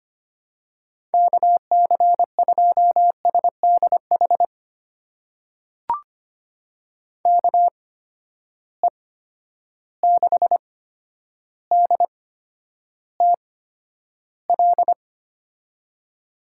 Morse code, heard as a keyed tone, about 25 wpm, spaced as for 11 wpm, 700 Hz. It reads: KC2SDH KE6DTL
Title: USA Call Signs - 6 Characters - 8x Character Spacing - 25wpm